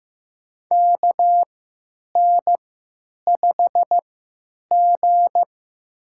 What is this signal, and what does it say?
Morse code, heard as a keyed tone, about 15 wpm, spaced as for 9 wpm, 700 Hz. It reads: KN5G
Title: USA Call Signs - 4 Characters - 3x Character Spacing - 15wpm